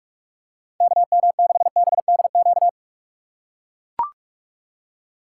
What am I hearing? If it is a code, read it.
Morse code, heard as a keyed tone, about 44 wpm, 700 Hz. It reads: KM6BDQ
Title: USA Call Signs - 6 Characters - 3x Character Spacing - 45wpm